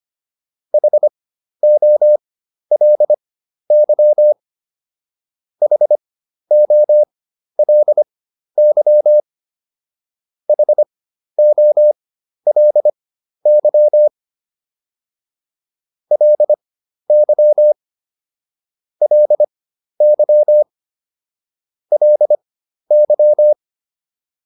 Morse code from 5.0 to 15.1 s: HOLY HOLY

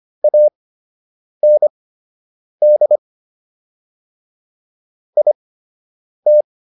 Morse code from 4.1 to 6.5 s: IT

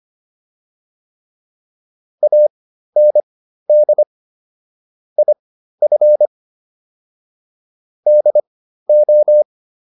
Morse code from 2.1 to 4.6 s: AND

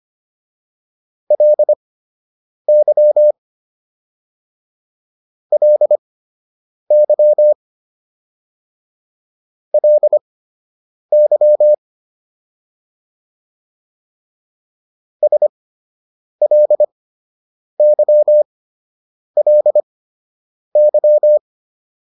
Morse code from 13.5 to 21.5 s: SLYLY

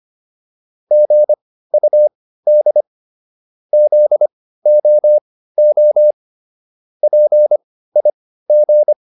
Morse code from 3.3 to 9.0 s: ZOO PIG

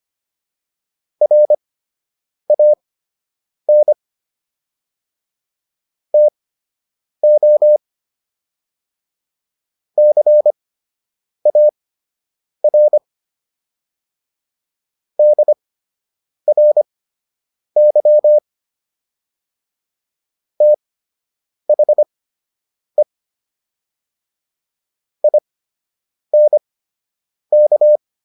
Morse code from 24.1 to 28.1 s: INK

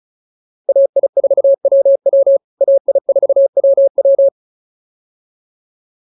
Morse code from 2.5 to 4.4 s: AI4WW

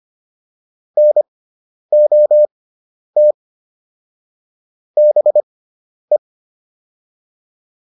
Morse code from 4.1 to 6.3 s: BE

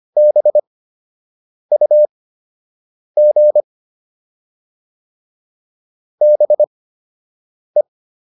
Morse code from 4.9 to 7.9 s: BE

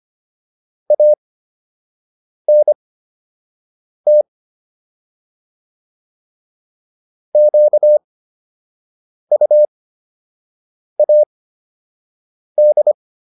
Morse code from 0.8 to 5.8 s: ANT

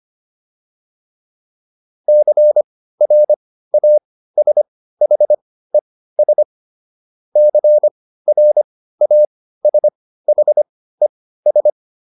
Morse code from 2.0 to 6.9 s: CRASHES

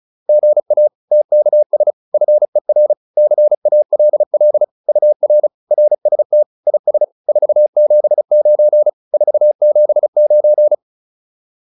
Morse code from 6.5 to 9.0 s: IS 479

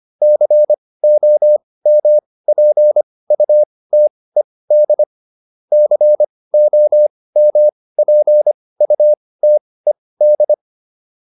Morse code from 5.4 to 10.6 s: COMPUTED

